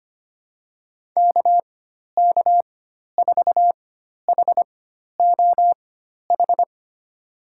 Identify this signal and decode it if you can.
Morse code, heard as a keyed tone, about 25 wpm, spaced as for 12 wpm, 700 Hz. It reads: KK4HOH
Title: USA Call Signs - 6 Characters - 4x Character Spacing - 25wpm